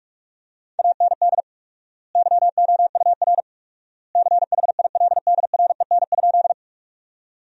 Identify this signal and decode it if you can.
Morse code, heard as a keyed tone, about 46 wpm, 700 Hz. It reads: AND YOUR CHILDREN?
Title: 4x Word-Spacing - Sentences from Top 300 Words - 45wpm